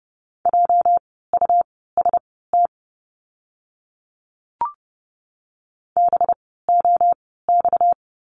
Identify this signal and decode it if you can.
Morse code, heard as a keyed tone, about 30 wpm, spaced as for 17 wpm, 700 Hz. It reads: JUST BOX
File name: course / 30:10wpm / 230 Lesson 37f - Letter x - Words 30wpm